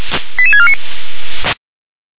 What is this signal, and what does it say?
Five-tone selective call (ZVEI), heard as quick runs of five tones, tones of about 70 ms. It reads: 88649